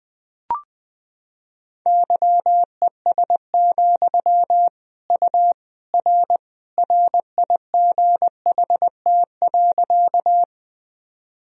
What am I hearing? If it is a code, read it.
Morse code, heard as a keyed tone, about 20 wpm, 700 Hz. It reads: YES, U R RIGHT.
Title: Sentences from Top 500 Words and Abbreviations 20wpm